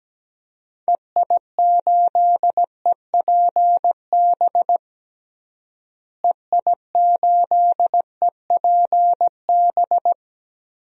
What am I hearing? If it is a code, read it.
Morse code, heard as a keyed tone, about 17 wpm, 700 Hz. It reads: EI8EPB EI8EPB
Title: DXCC-100 Call Signs - 6 Characters - Repeated 2x - 17wpm